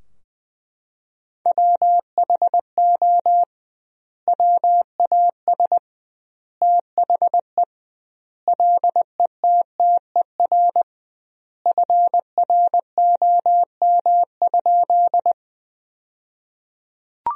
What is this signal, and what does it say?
Morse code, heard as a keyed tone, about 20 wpm, 700 Hz. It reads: WHO WAS THE LETTER FROM?
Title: 2x Word-Spacing - Sentences from Top 300 Words - 20wpm